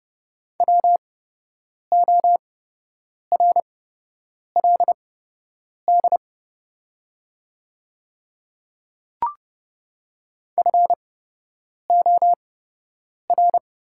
Morse code, heard as a keyed tone, about 30 wpm, spaced as for 8 wpm, 700 Hz. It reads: WORLD FOR
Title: 8x Character Spacing - Top 300 Words - 30wpm